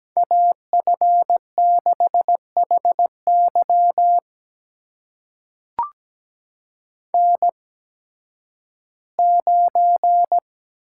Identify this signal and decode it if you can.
Morse code, heard as a keyed tone, about 17 wpm, spaced as for 11 wpm, 700 Hz. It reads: AF6HY N9
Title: USA Call Signs - 5 Characters - 8x Character Spacing - 17wpm